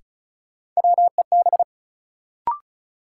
Morse code, heard as a keyed tone, about 36 wpm, 700 Hz. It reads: WEB
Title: Sentences from Top 1000 Words - Review Part 4 - New Word Drill - 35wpm